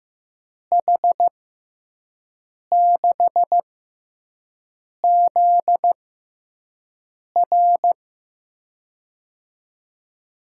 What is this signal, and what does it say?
Morse code, heard as a keyed tone, about 15 wpm, spaced as for 5 wpm, 700 Hz. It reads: H6ZR
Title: USA Call Signs - 5 Characters - 6x Character Spacing - 15wpm